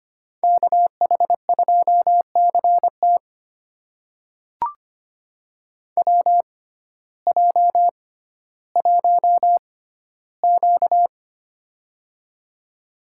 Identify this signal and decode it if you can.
Morse code, heard as a keyed tone, about 25 wpm, spaced as for 14 wpm, 700 Hz. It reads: KH2CT WJ1Q
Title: USA Call Signs - 6x Character Spacing - 25wpm